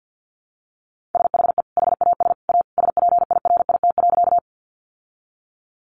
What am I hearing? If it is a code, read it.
Morse code, heard as a keyed tone, about 50 wpm, 700 Hz. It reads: SHE HAS A SPIRIT.
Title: Sentences from Top 1400 Words 50wpm